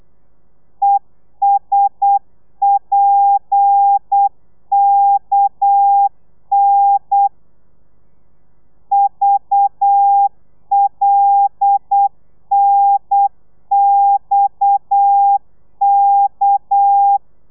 Morse code, heard as a keyed tone, about 8 wpm, 800 Hz. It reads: ESPKN VLNXK